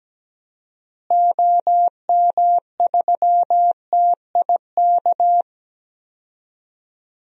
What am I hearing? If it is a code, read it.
Morse code, heard as a keyed tone, about 17 wpm, 700 Hz. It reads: OM3TIK
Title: DXCC-100 Call Signs - 6 Characters - Repeated 3x - 17wpm